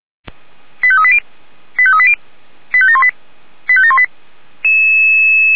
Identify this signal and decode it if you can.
Five-tone selective call (ZVEI), heard as quick runs of five tones, tones of about 70 ms. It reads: 85380 85380 85718 85718